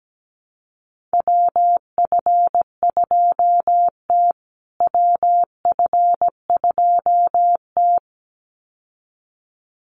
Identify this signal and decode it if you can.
Morse code, heard as a keyed tone, about 17 wpm, 700 Hz. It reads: WF2T WF2T
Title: USA Call Signs Repeated 2x 17wpm